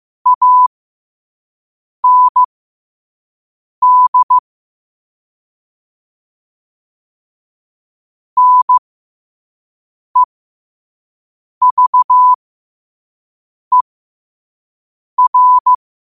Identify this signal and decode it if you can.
Morse code, heard as a keyed tone, about 15 wpm, spaced as for 5 wpm, 1000 Hz. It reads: AND NEVER
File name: qotdfiles / 5wpm